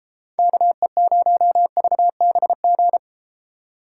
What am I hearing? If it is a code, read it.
Morse code, heard as a keyed tone, about 32 wpm, 700 Hz. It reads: KE0VBG